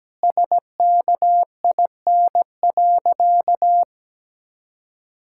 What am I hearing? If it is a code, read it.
Morse code, heard as a keyed tone, about 17 wpm, 700 Hz. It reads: SKIN.
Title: Sentences from Top 1600 Words 17wpm